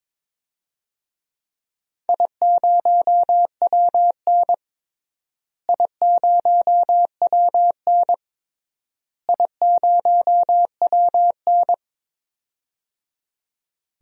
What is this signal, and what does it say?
Morse code, heard as a keyed tone, about 22 wpm, 700 Hz. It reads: I0WN I0WN I0WN
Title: DXCC-100 Call Signs - 4 Characters - Repeated 3x - 22wpm